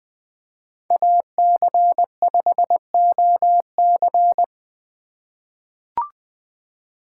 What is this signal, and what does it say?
Morse code, heard as a keyed tone, about 20 wpm, 700 Hz. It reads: AC5OC
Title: 2023 POTA Hunter Call Signs - 20wpm